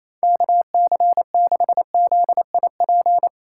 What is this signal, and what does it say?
Morse code, heard as a keyed tone, about 35 wpm, 700 Hz. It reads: KC6ZIP
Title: Speed Racing - USA 2x3 Call Signs 28wpm